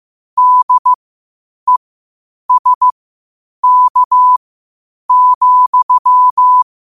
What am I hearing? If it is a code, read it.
Morse code, heard as a keyed tone, about 15 wpm, spaced as for 8 wpm, 1000 Hz. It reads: DESK,
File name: qotdfiles / 8wpm